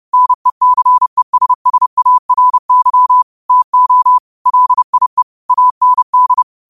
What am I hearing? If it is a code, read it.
Morse code, heard as a keyed tone, about 30 wpm, 1000 Hz. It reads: NECESSARY TO LIE AND